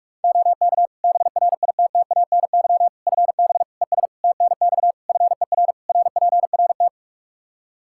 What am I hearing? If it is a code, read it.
Morse code, heard as a keyed tone, about 46 wpm, 700 Hz. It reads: OK BRITTANY FB ES TNX FER RPRT
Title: Practice QSO - Exchange 2 - Info - 45wpm